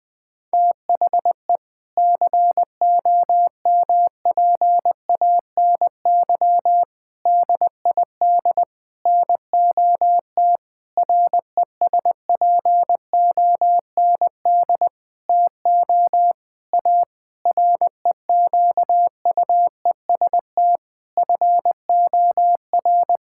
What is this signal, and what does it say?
Morse code, heard as a keyed tone, about 21 wpm, 700 Hz. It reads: THE COMPANY DID NOT RESPOND TO A REQUEST FOR
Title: Sentences with 8-Word Phrases - 20wpm